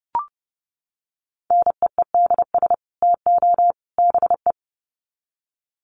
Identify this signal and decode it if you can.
Morse code, heard as a keyed tone, about 30 wpm, 700 Hz. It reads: NEEDS TO BE
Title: Sets of 3 Words 30wpm